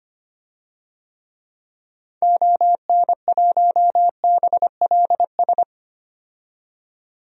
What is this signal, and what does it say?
Morse code, heard as a keyed tone, about 25 wpm, 700 Hz. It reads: ON1BLS
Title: DXCC-100 Call Signs - 6 Characters - 25wpm